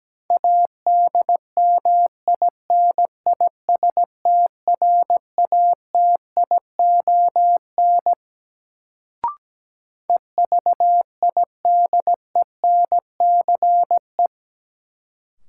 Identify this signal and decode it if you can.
Morse code, heard as a keyed tone, about 17 wpm, 700 Hz. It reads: ADMINISTRATION EVIDENCE
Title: Sentences from Top 1300 Words - New Word Drill - 17wpm